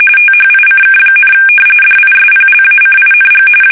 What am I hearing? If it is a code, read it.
RTTY, about 45.45 baud. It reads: THICK, DISPLACING
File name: fsk850 / battleships